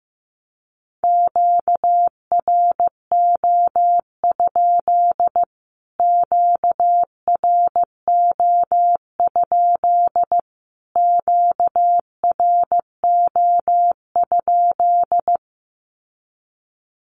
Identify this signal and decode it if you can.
Morse code, heard as a keyed tone, about 15 wpm, 700 Hz. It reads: QRO? QRO? QRO?
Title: Q-Codes Repeated 3x 15wpm